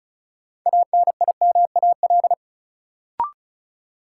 Morse code, heard as a keyed tone, about 36 wpm, 700 Hz. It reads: ANIMAL
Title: Sentences from Top 1000 Words - Review Part 1 - New Word Drill - 35wpm